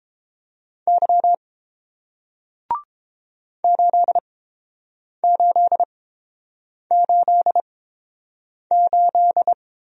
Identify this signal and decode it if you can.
Morse code, heard as a keyed tone, about 30 wpm, spaced as for 7 wpm, 700 Hz. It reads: Y 8888